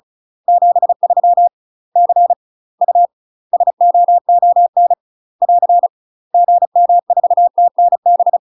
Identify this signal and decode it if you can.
Morse code, heard as a keyed tone, about 36 wpm, 700 Hz. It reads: 73 C U SOON <AR> GM4TNB
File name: UK MASTER QSOs TRAINING / 35WPM / UK MASTER QSOs 0014